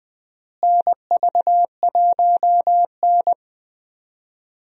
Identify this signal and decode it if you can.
Morse code, heard as a keyed tone, about 20 wpm, 700 Hz. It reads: NV1N
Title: USA Call Signs - 4 Characters - Repeated 3x - 20wpm